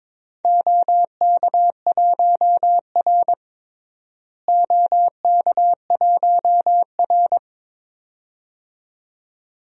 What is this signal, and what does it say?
Morse code, heard as a keyed tone, about 22 wpm, 700 Hz. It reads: OK1R OK1R
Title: DXCC-100 Call Signs - 4 Characters - Repeated 2x - 22wpm